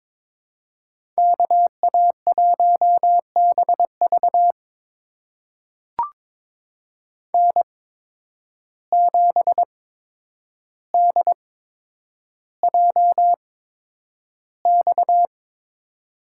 Morse code, heard as a keyed tone, about 22 wpm, spaced as for 9 wpm, 700 Hz. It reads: KA1BV N7DJX
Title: USA Call Signs - 5 Characters - 8x Character Spacing - 22wpm